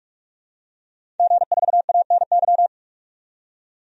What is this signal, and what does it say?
Morse code, heard as a keyed tone, about 42 wpm, 700 Hz. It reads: G4ANY
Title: DXCC-100 Call Signs - 5 Characters - 45wpm